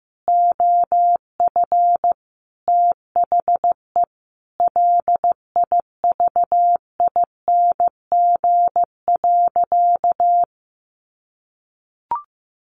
Morse code, heard as a keyed tone, about 15 wpm, 700 Hz. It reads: OF THE LIVING.
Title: Sentences from Top 1100 Words 15wpm